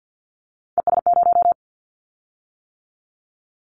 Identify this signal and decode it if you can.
Morse code, heard as a keyed tone, about 50 wpm, 700 Hz. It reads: ES0
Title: Callsign Prefixes for Top 100 DXCC - 50wpm